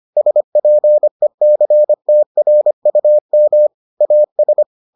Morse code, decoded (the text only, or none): SPECTRUM AS